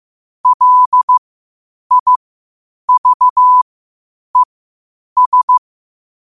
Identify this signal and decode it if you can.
Morse code, heard as a keyed tone, about 15 wpm, spaced as for 8 wpm, 1000 Hz. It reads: LIVES
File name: qotdfiles / 8wpm